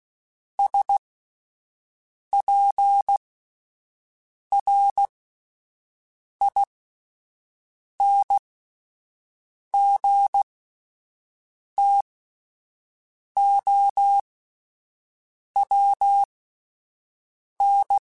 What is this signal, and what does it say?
Morse code, heard as a keyed tone, about 16 wpm, spaced as for 5 wpm, 800 Hz. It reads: SPRINGTOWN